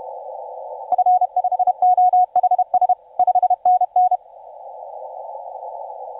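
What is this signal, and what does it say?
Morse code, heard as a keyed tone, about 31 wpm, 730 Hz. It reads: F5OHS 5NN
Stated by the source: TU7C on 20 metre CW, DX station recording by IZ8GNR.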